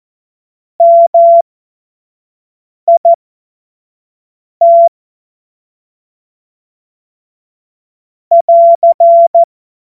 Morse code, heard as a keyed tone, about 14 wpm, spaced as for 5 wpm, 680 Hz. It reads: MIT <AR>
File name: Morsekurs / Witze / Witz 16. Dezember 2025 25 F70